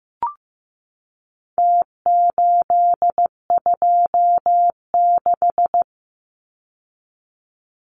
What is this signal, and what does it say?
Morse code, heard as a keyed tone, about 15 wpm, 700 Hz. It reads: T826
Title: Numbers - Contest Style - 4-Digit 15wpm